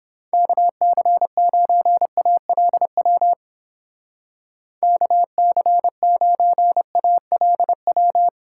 Morse code, heard as a keyed tone, about 28 wpm, 700 Hz. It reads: KC9ALW KC9ALW